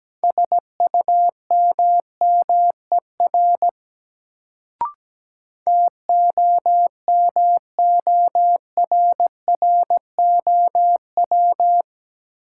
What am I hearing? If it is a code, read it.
Morse code, heard as a keyed tone, about 17 wpm, 700 Hz. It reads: SUMMER TOMORROW
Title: Sentences from Top 1100 Words - New Word Drill Alt - 17wpm